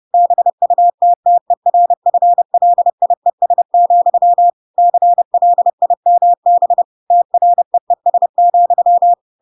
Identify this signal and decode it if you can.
Morse code, heard as a keyed tone, about 30 wpm, 700 Hz. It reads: BUTTERFLIES, CLIMB TREES,
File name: Pinocchio / Pinocchio ch4 30 30 ik1hge